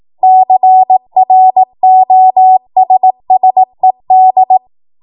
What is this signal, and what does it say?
Morse code, heard as a keyed tone, about 18 wpm, 750 Hz. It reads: CROSSED
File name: Morse / 18WPM